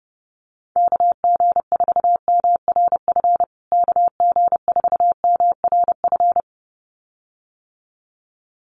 Morse code, heard as a keyed tone, about 30 wpm, 700 Hz. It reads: KG4MRF KG4MRF
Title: USA Call Signs Repeated 2x 30wpm